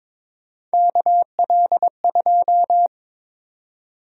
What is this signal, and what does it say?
Morse code, heard as a keyed tone, about 22 wpm, 700 Hz. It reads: KL2
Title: USA Call Sign Prefixes - 22wpm